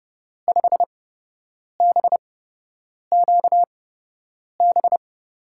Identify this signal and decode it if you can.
Morse code, heard as a keyed tone, about 30 wpm, spaced as for 8 wpm, 700 Hz. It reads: 5BQB
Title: USA Call Signs - 6 Characters - 8x Character Spacing - 30wpm